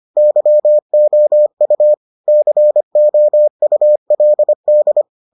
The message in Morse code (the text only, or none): YOU COULD